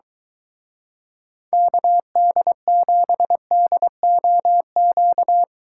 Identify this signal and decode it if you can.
Morse code, heard as a keyed tone, about 22 wpm, 700 Hz. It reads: KD7DOQ